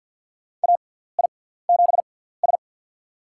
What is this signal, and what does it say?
Morse code, heard as a keyed tone, about 47 wpm, spaced as for 17 wpm, 700 Hz. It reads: AI7S
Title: USA Call Signs - 4 Characters - 6x Character Spacing - 50wpm